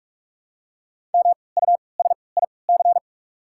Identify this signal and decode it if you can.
Morse code, heard as a keyed tone, about 47 wpm, spaced as for 26 wpm, 690 Hz. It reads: MUSIC
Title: 3x Character Spacing - Top 300 Words - 45wpm